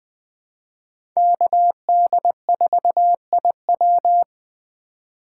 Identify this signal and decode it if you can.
Morse code, heard as a keyed tone, about 20 wpm, 700 Hz. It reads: KD4IW
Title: USA Call Signs - 5 Characters - 3x Character Spacing - 20wpm